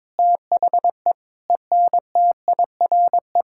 Morse code, heard as a keyed tone, about 22 wpm, 700 Hz. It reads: THE ENTIRE
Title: Sentences from Top 1000 Words - Review Part 5 - 22wpm